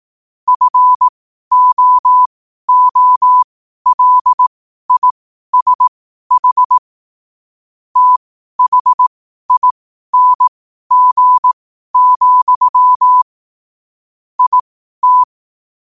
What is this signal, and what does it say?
Morse code, heard as a keyed tone, about 18 wpm, spaced as for 12 wpm, 1000 Hz. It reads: FOOLISH THING, IT
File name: qotdfiles / 12wpm